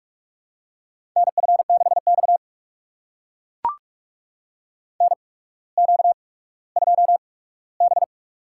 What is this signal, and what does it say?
Morse code, heard as a keyed tone, about 42 wpm, spaced as for 19 wpm, 700 Hz. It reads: NF6X NQ2B
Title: USA Call Signs - 4 Characters - 8x Character Spacing - 45wpm